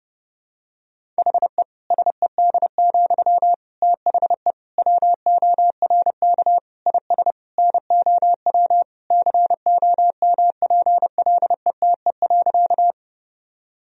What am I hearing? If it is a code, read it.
Morse code, heard as a keyed tone, about 30 wpm, 700 Hz. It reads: HE SED, THE WORK IS NOW COMPLETE.
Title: Sentences from Top 400 Words and Abbreviations 30wpm